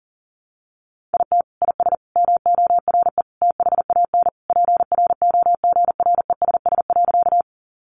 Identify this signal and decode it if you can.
Morse code, heard as a keyed tone, about 40 wpm, 700 Hz. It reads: IT IS MORE THAN PROGRESS.
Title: Sentences from Top 1400 Words 40wpm